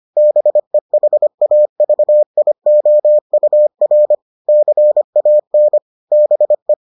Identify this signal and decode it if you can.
Morse code, heard as a keyed tone, about 25 wpm, 600 Hz. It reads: BEHAVIOUR CAN BE